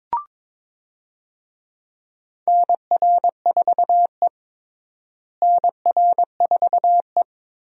Morse code, heard as a keyed tone, about 22 wpm, 700 Hz. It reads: NR4E NR4E
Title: USA Call Signs - 4 Characters - Repeated 3x - 22wpm